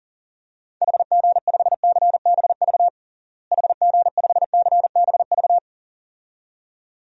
Morse code, heard as a keyed tone, about 40 wpm, 700 Hz. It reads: HG5CBV HG5CBV
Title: DXCC-100 Call Signs - 6 Characters - Repeated 2x - 40wpm